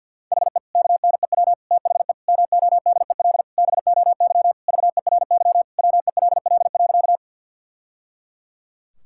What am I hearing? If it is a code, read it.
Morse code, heard as a keyed tone, about 52 wpm, 700 Hz. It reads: HE KNEW THE MODEL BOY VERY WELL.
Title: Sentences from Top 1000 Words - Review Part 5 - 50wpm